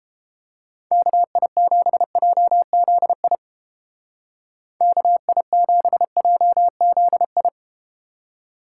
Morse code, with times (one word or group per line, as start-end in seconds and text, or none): 0.9-3.4 KI7JZI
4.8-7.5 KI7JZI